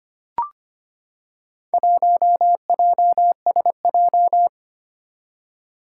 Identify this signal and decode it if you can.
Morse code, heard as a keyed tone, about 25 wpm, 700 Hz. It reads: 1JSJ
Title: Four Challenging Letter-Number Combinations 25wpm